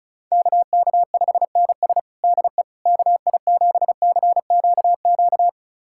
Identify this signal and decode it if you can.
Morse code, heard as a keyed tone, about 36 wpm, 700 Hz. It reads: KK5NS DE KI7CQQ
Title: USA Callsign DE Callsign 35wpm